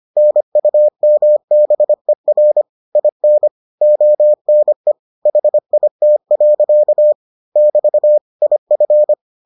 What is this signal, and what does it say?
Morse code, heard as a keyed tone, about 25 wpm, 600 Hz. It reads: NUMBER IN ONE HIT. <BT> IF